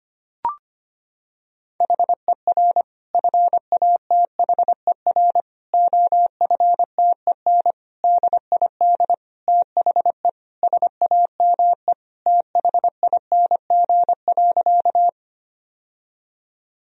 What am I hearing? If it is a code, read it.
Morse code, heard as a keyed tone, about 25 wpm, 700 Hz. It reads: HER FATHER OFTEN DID THE SAME THING.
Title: Sentences from Top 1000 Words - Review Part 1 - 25wpm